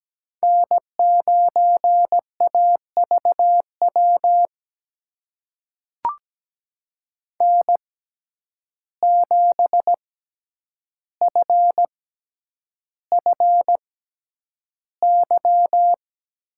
Morse code, heard as a keyed tone, about 17 wpm, spaced as for 9 wpm, 700 Hz. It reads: N9AVW N7FFY
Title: USA Call Signs - 5 Characters - 6x Character Spacing - 17wpm